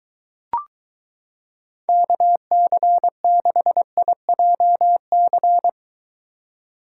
Morse code, heard as a keyed tone, about 23 wpm, 700 Hz. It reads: KC6IJC